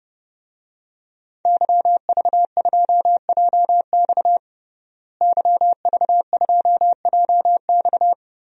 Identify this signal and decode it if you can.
Morse code, heard as a keyed tone, about 30 wpm, 700 Hz. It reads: YV2JX YV2JX
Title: DXCC-100 Call Signs - 5 Characters - Repeated 3x - 30wpm